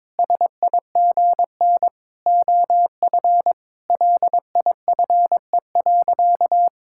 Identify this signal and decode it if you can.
Morse code, heard as a keyed tone, about 22 wpm, 700 Hz. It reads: SIGN OF LIFE.
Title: Sentences from Top 1000 Words - Review Part 6 - 22wpm